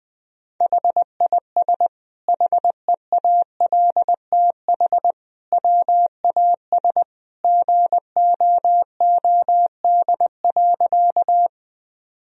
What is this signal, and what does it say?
Morse code, heard as a keyed tone, about 20 wpm, 700 Hz. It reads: HIS HEALTH WAS GOOD.
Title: Sentences from Top 1000 Words - Review Part 3 - 20wpm